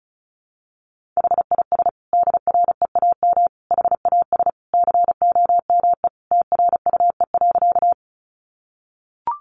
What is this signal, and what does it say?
Morse code, heard as a keyed tone, about 34 wpm, 700 Hz. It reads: HIS DREAM HAS COME TRUE.
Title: Sentences from Top 1500 Words 35wpm